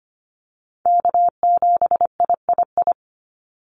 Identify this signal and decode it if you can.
Morse code, heard as a keyed tone, about 25 wpm, 700 Hz. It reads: K7III
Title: NA SOTA Chaser Callsigns 25wpm